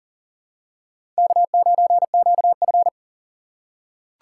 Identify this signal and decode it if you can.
Morse code, heard as a keyed tone, about 40 wpm, 700 Hz. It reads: K9QF
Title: USA Call Signs - 2x Character Spacing - 40wpm